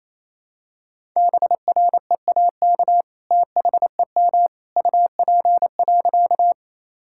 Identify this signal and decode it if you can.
Morse code, heard as a keyed tone, about 28 wpm, 700 Hz. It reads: BREAK THEM UP.
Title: Sentences from Top 1000 Words - Review Part 6 - 28wpm